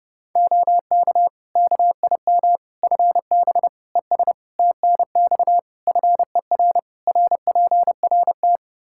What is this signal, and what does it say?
Morse code, heard as a keyed tone, about 30 wpm, 700 Hz. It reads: OK KIM FB ES TNX FER RPRT
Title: Practice QSO - Exchange 2 - Info - 30wpm